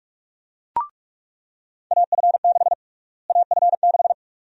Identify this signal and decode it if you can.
Morse code, heard as a keyed tone, about 46 wpm, 700 Hz. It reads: AF6 AF6
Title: USA Call Sign Prefixes - Repeated 2x - 45wpm